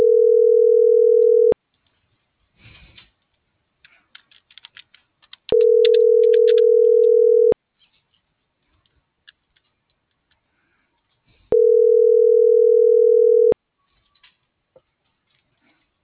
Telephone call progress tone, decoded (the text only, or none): ringback tone